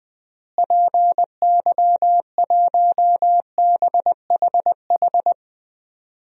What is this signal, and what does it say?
Morse code, heard as a keyed tone, about 20 wpm, 700 Hz. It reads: PY1BHH
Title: DXCC-100 Call Signs - 6 Characters - Repeated 2x - 20wpm